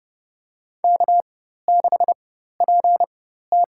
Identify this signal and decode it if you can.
Morse code, heard as a keyed tone, about 32 wpm, spaced as for 14 wpm, 700 Hz. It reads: K6PT
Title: USA Call Signs - 4 Characters - 4x Character Spacing - 30wpm